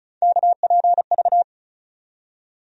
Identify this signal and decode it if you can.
Morse code, heard as a keyed tone, about 34 wpm, 700 Hz. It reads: KPV
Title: USA Call Sign Postfixes - 3 Letters 35wpm